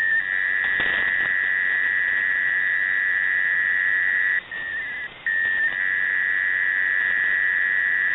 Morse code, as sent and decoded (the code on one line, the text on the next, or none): none